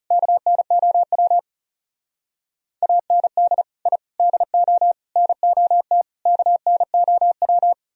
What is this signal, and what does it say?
Morse code, heard as a keyed tone, about 34 wpm, 700 Hz. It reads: KNOW AND I DO NOT KNOW
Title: Speed-Racing - Sets of 5 Words 30wpm